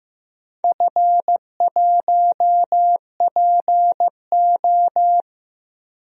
Morse code, heard as a keyed tone, about 15 wpm, 700 Hz. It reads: F1PO
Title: DXCC-100 Call Signs - 4 Characters - 15wpm